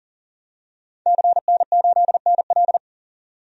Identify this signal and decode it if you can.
Morse code, heard as a keyed tone, about 40 wpm, 700 Hz. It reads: CN8NL
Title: DXCC-100 Call Signs - 5 Characters - 40wpm